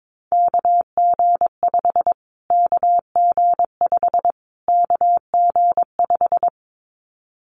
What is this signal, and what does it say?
Morse code, heard as a keyed tone, about 22 wpm, 700 Hz. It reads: KG5 KG5 KG5
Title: Callsign Prefixes for Top 25 DXCC - Repeated 3x - 22wpm